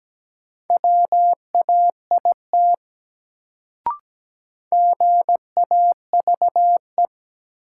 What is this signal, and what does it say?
Morse code, heard as a keyed tone, about 17 wpm, 700 Hz. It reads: WAIT GAVE
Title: Sentences from Top 500 Words - New Word Drill - 17wpm